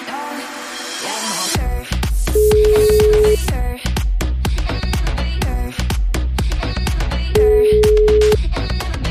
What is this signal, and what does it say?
Ringback tone (Europe, 425 Hz).